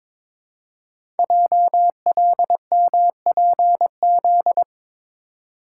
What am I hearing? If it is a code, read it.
Morse code, heard as a keyed tone, about 22 wpm, 700 Hz. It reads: JLMPZ
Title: Code Groups - 5 Letters - 22wpm